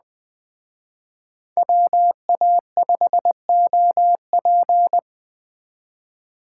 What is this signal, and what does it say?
Morse code, heard as a keyed tone, about 20 wpm, 700 Hz. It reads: WA5OP